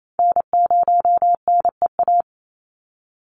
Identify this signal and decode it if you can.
Morse code, heard as a keyed tone, about 28 wpm, 700 Hz. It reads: N0NEA
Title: USA Call Signs 28wpm